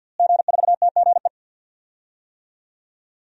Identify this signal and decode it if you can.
Morse code, heard as a keyed tone, about 53 wpm, 700 Hz. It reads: G4TGE
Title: DXCC-100 Call Signs - 5 Characters - Repeated 3x - 50wpm